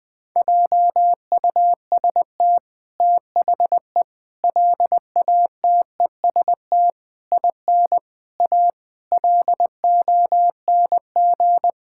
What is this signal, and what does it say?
Morse code, heard as a keyed tone, about 21 wpm, 700 Hz. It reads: JUST THE LATEST IN A LONG
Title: Sentences with 8-Word Phrases - 20wpm